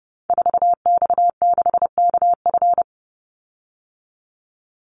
Morse code, heard as a keyed tone, about 30 wpm, 700 Hz. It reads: 4X6KF
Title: Call Signs Repeated 3x - 30wpm